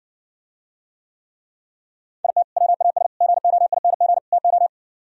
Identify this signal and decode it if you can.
Morse code, heard as a keyed tone, about 64 wpm, 700 Hz. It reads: IT WAS GOING TO